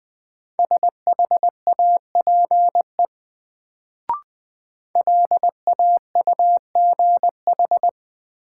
Morse code, heard as a keyed tone, about 21 wpm, 700 Hz. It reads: SHAPE LAUGH
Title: Sentences from Top 500 Words - New Word Drill - 20wpm